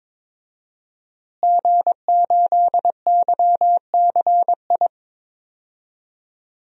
Morse code, heard as a keyed tone, about 22 wpm, 700 Hz. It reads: G8YCI
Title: DXCC-100 Call Signs - 5 Characters - Repeated 2x - 22wpm